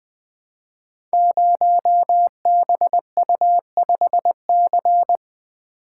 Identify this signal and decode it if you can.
Morse code, heard as a keyed tone, about 20 wpm, 700 Hz. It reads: 0BU5C
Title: Code Groups - 5 Letter:Numbers - 20wpm